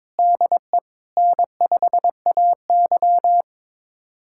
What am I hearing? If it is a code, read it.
Morse code, heard as a keyed tone, about 22 wpm, 700 Hz. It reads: DE N5AY
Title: USA Callsign DE Callsign - 4 Characters - 22wpm